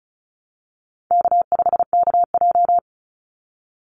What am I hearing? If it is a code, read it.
Morse code, heard as a keyed tone, about 34 wpm, 700 Hz. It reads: K5KJ
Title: Easy Call Signs Repeated 2x - 35wpm